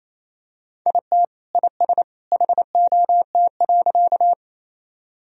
Morse code, heard as a keyed tone, about 28 wpm, 700 Hz. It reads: IT IS HOT.
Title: Sentences from Top 1000 Words - Review Part 3 - 28wpm